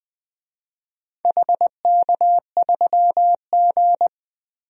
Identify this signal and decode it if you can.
Morse code, heard as a keyed tone, about 20 wpm, 700 Hz. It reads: HK3G
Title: DXCC-100 Call Signs - 4 Characters - Repeated 3x - 20wpm